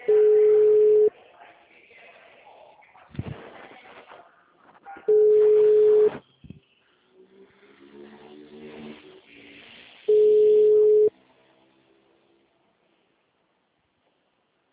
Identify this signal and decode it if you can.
Ringback tone (Europe, 425 Hz), with speech.